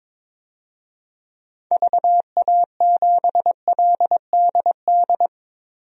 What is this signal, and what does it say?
Morse code, heard as a keyed tone, about 22 wpm, 700 Hz. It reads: VA7LDD
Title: DXCC-100 Call Signs - 6 Characters - 22wpm